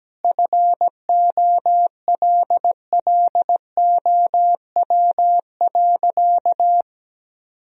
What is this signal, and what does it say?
Morse code, heard as a keyed tone, about 17 wpm, 700 Hz. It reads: FOLLOW.
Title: Sentences from Top 1000 Words - Review Part 1 - 17wpm